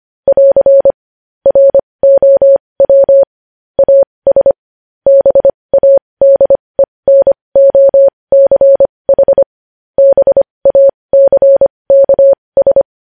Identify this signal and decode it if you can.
Morse code, heard as a keyed tone, about 25 wpm, spaced as for 20 wpm, 550 Hz. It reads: <AR> ROW AS BADENOCH BACKS
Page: news headlines in morse code at 20 wpm